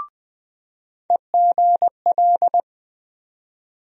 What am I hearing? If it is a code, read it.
Morse code, heard as a keyed tone, about 20 wpm, 700 Hz. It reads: EGL
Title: USA Call Sign Postfixes - 3 Letters 20wpm